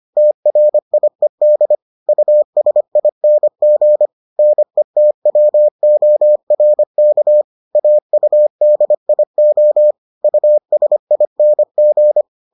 Morse code, heard as a keyed tone, about 25 wpm, 600 Hz. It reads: TRIED USING NETWORK AUDIO USING